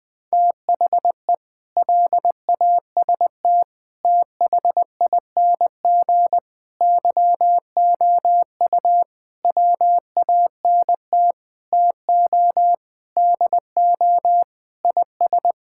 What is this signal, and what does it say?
Morse code, heard as a keyed tone, about 21 wpm, 700 Hz. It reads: THE LAST THING YOU WANT TO DO IS
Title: Sentences with 8-Word Phrases - Spoken Beginning - 20wpm